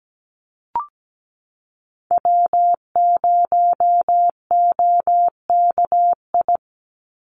Morse code, heard as a keyed tone, about 17 wpm, 700 Hz. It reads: W0OKI
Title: USA Call Signs 17wpm